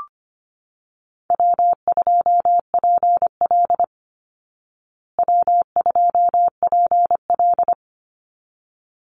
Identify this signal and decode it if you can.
Morse code, heard as a keyed tone, about 25 wpm, 700 Hz. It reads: W2PL W2PL
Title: Easy Call Signs Repeated 2x - 25wpm